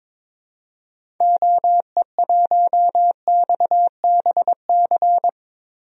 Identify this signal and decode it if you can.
Morse code, heard as a keyed tone, about 23 wpm, 700 Hz. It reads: OE1XBC
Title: DXCC-100 Call Signs - 6 Characters - 22wpm